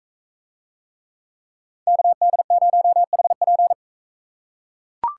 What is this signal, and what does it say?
Morse code, heard as a keyed tone, about 40 wpm, 700 Hz. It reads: KD0HP